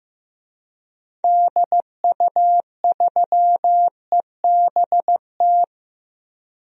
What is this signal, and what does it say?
Morse code, heard as a keyed tone, about 16 wpm, 700 Hz. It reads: DU3EBT
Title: DXCC-100 Call Signs - 6 Characters - Repeated 2x - 15wpm